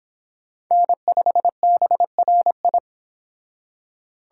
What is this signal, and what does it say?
Morse code, heard as a keyed tone, about 29 wpm, 700 Hz. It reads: N5BRI